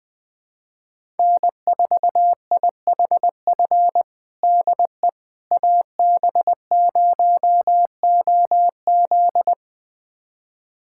Morse code, heard as a keyed tone, about 20 wpm, 700 Hz. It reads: N4IHF DE AB0OZ
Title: USA Callsign DE Callsign - 5 Characters - 20wpm